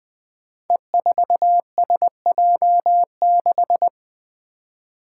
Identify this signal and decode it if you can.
Morse code, heard as a keyed tone, about 20 wpm, 700 Hz. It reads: E4SJ6
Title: Five Challenging Letter-Number Combinations 20wpm